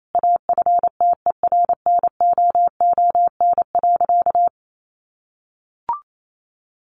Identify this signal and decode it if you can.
Morse code, heard as a keyed tone, about 28 wpm, 700 Hz. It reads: AFTERNOON.
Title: Sentences from Top 1500 Words 28wpm